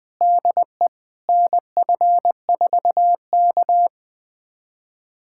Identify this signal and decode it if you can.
Morse code, heard as a keyed tone, about 20 wpm, 700 Hz. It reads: DE NF4K
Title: USA Callsign DE Callsign - 4 Characters - 20wpm